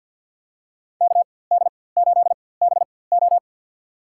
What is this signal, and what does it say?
Morse code, heard as a keyed tone, about 47 wpm, spaced as for 23 wpm, 700 Hz. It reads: KD8BO
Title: USA Call Signs - 4x Character Spacing - 50wpm